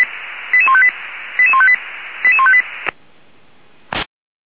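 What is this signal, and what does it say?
Five-tone selective call (ZVEI), heard as quick runs of five tones, tones of about 70 ms. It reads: 88158 88158 88158
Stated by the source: ZVEI five-tone call received on SDR, channel 464.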